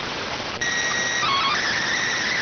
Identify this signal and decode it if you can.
SSTV (slow-scan television), PD 290.